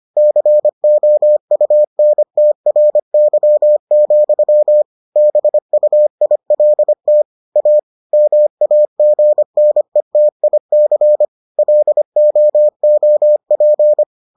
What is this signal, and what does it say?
Morse code, heard as a keyed tone, about 25 wpm, 600 Hz. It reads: COUNTRY, BUILT A MAGNETIC LOOP